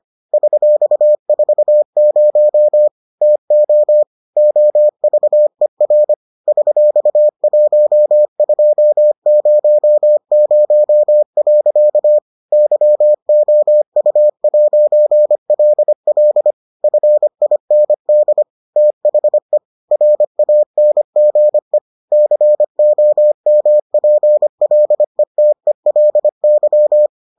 Morse code, heard as a keyed tone, about 25 wpm, 600 Hz. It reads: $40 TO OVER $1200. YOU'LL FIND THE RANGE COMPLETELY